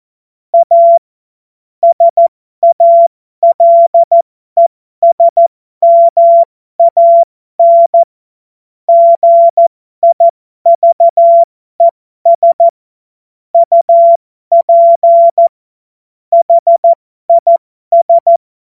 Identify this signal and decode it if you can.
Morse code, heard as a keyed tone, about 14 wpm, spaced as for 12 wpm, 680 Hz. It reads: A SALESMAN GIVES UP HIS
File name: Morsekurs / Witze